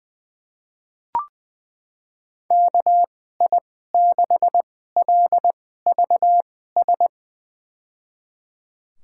Morse code, heard as a keyed tone, about 20 wpm, spaced as for 14 wpm, 700 Hz. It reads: KI6LVS
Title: USA Call Signs - 2x Character Spacing - 20wpm